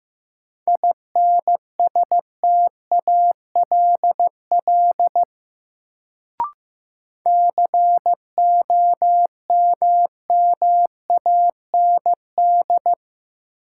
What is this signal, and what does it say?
Morse code, heard as a keyed tone, about 15 wpm, 700 Hz. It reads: INSTALL COMMAND
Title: Sentences from Top 1600 Words - New Word Drill - 15wpm